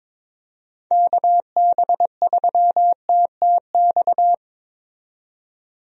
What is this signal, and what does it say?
Morse code, heard as a keyed tone, about 21 wpm, 700 Hz. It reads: KB3TTX
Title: USA Call Signs - 6 Characters - 22wpm